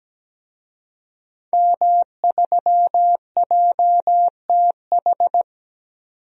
Morse code, heard as a keyed tone, about 17 wpm, 700 Hz. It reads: M3JTH
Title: DXCC-100 Call Signs - 5 Characters - 17wpm